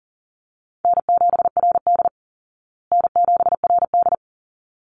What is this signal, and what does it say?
Morse code, heard as a keyed tone, about 38 wpm, 700 Hz. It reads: N7RD N7RD
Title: Easy Call Signs Repeated 2x - 40wpm